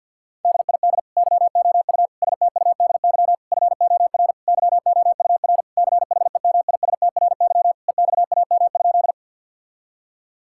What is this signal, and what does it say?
Morse code, heard as a keyed tone, about 50 wpm, 700 Hz. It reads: DID YOU STUDY FOR YOUR CHEMISTRY EXAM?
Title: Sentences from Top 1000 Words - Review Part 6 - 50wpm